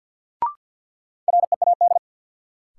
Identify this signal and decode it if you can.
Morse code, heard as a keyed tone, about 51 wpm, 700 Hz. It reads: READ
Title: Sentences from Top 200 Words - New Word Drill - 50wpm